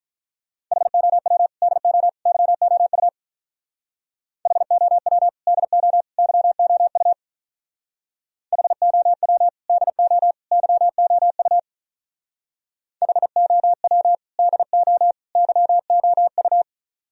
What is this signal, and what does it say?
Morse code, heard as a keyed tone, about 43 wpm, 700 Hz. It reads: HOW DO YOU HOW DO YOU HOW DO YOU HOW DO YOU
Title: Speed-Racing - Sets of 3 Words 35wpm